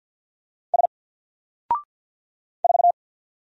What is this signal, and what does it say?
Morse code, heard as a keyed tone, about 50 wpm, 700 Hz. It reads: S 4
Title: Single Character (Letter:Number:Punctuation:Prosign) 50wpm